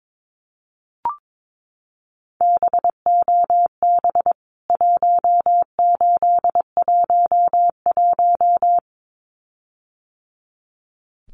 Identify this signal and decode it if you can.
Morse code, heard as a keyed tone, about 22 wpm, 700 Hz. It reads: BOB 1811
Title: CWT Exchange 22wpm